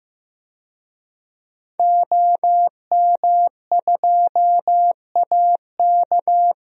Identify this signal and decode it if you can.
Morse code, heard as a keyed tone, about 15 wpm, 700 Hz. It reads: OM2AK
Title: DXCC-100 Call Signs - 5 Characters - Repeated 2x - 15wpm